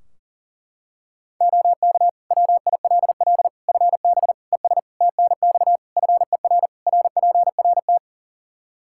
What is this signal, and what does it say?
Morse code, heard as a keyed tone, about 40 wpm, 700 Hz. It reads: OK WILL FB ES TNX FER RPRT
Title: Practice QSO - Exchange 2 - Info - 40wpm